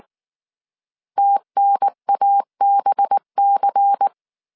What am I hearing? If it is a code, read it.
Morse code, heard as a keyed tone, about 19 wpm, 800 Hz. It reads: TNABC
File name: CW / DET / 1 6 1500C S C 18wpm 2356